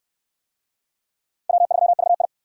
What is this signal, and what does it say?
Morse code, heard as a keyed tone, about 64 wpm, 700 Hz. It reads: W3VI